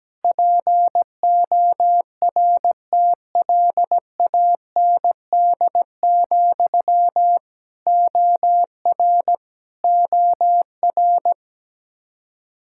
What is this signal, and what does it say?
Morse code, heard as a keyed tone, about 17 wpm, 700 Hz. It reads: PORTLAND, OR OR
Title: Practice QSO - Exchange 1 - Beginning - 17wpm